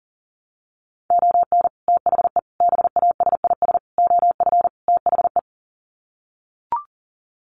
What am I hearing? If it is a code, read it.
Morse code, heard as a keyed tone, about 40 wpm, 700 Hz. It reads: ON THE BASIS OF THE
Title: Sets of 5 Words 40wpm